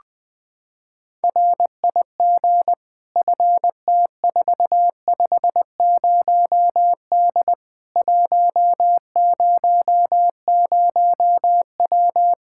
Morse code, heard as a keyed tone, about 21 wpm, 700 Hz. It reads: RIG FT450D 100W
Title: K1USN SST QSO Practice 20wpm